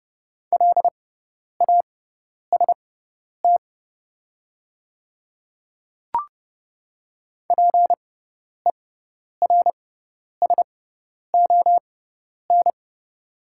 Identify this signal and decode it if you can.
Morse code, heard as a keyed tone, about 30 wpm, spaced as for 10 wpm, 700 Hz. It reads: LAST PERSON
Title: 6x Character Spacing - Top 500 Words - 30wpm